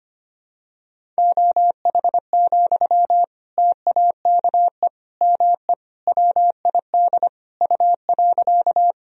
Morse code, heard as a keyed tone, about 25 wpm, 700 Hz. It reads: OH, TAKE ME WID U.
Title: Sentences from Top 500 Words and Abbreviations 25wpm